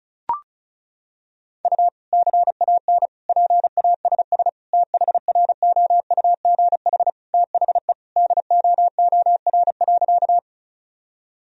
Morse code, heard as a keyed tone, about 36 wpm, 700 Hz. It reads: U CAN PASS THROUGH THE DOOR.
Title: Sentences from Top 400 Words and Abbreviations 35wpm